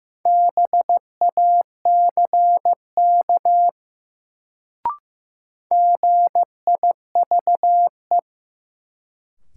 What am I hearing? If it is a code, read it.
Morse code, heard as a keyed tone, about 15 wpm, 700 Hz. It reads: BACK GIVE
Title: Sentences from Top 200 Words - New Word Drill - 15wpm